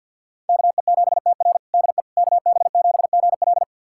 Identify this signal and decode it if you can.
Morse code, heard as a keyed tone, about 47 wpm, 700 Hz. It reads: KE7TR DE KB7GL
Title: USA Callsign DE Callsign - 5 Characters - 50wpm